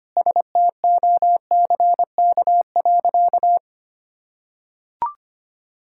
Morse code, heard as a keyed tone, about 25 wpm, 700 Hz. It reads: STOCK.
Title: Sentences from Top 1000 Words - Review Part 5 - 25wpm